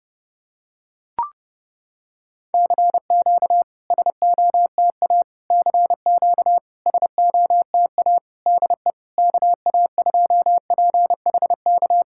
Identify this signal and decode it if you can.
Morse code, heard as a keyed tone, about 30 wpm, 700 Hz. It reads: CQ SOTA CQ SOTA DE KA2PHK
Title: SOTA QSO Practice 30wpm